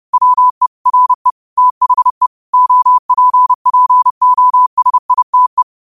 Morse code, heard as a keyed tone, about 30 wpm, 1000 Hz. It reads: WE RE THE OPPOSITE
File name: qotdfiles / 30wpm